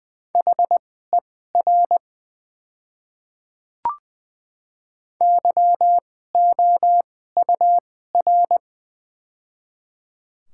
Morse code, heard as a keyed tone, about 20 wpm, spaced as for 14 wpm, 700 Hz. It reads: HER YOUR
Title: 2x Character Spacing - Top 100 Words - 20wpm